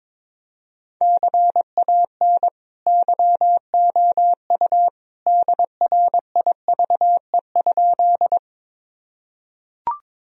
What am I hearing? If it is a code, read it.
Morse code, heard as a keyed tone, about 22 wpm, 700 Hz. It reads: CAN YOU DRIVE?
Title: Sentences from Top 1000 Words - Review Part 2 - 22wpm